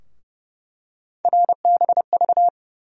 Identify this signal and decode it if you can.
Morse code, heard as a keyed tone, about 30 wpm, 700 Hz. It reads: RBV
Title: USA Call Sign Postfixes - 3 Letters 30wpm